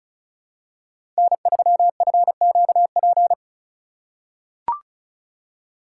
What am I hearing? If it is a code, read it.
Morse code, heard as a keyed tone, about 36 wpm, 700 Hz. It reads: N3FQP
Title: USA Call Signs - 5 Characters - 4x Character Spacing - 35wpm